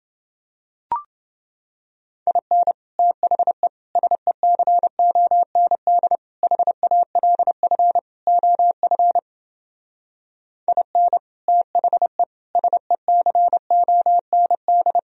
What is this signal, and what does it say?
Morse code, heard as a keyed tone, about 28 wpm, 700 Hz. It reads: IN THE SECOND HALF OF IN THE SECOND